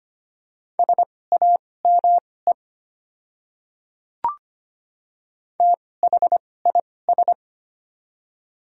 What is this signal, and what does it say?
Morse code, heard as a keyed tone, about 25 wpm, spaced as for 18 wpm, 700 Hz. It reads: SAME THIS
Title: 2x Character Spacing - Top 300 Words - 25wpm